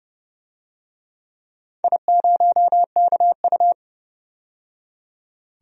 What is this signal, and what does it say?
Morse code, heard as a keyed tone, about 30 wpm, 700 Hz. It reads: I0KU
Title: DXCC-100 Call Signs - 4 Characters - 30wpm